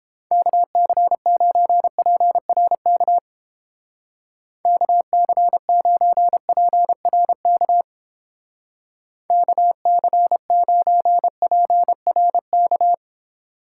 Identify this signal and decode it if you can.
Morse code, heard as a keyed tone, about 29 wpm, 700 Hz. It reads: KC9PRK KC9PRK KC9PRK